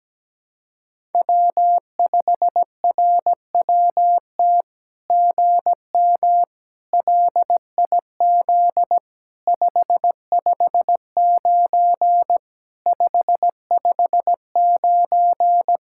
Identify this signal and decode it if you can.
Morse code, heard as a keyed tone, about 17 wpm, 700 Hz. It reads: W5RWT GM LIZ 559 559
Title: SOTA QSO Practice 17wpm